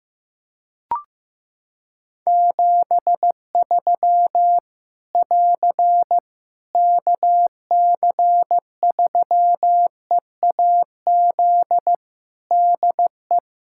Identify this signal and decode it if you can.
Morse code, heard as a keyed tone, about 16 wpm, 700 Hz. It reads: 73 <AR> KC3EAZ DE
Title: Practice QSO - Exchange 4 - Ending - 15wpm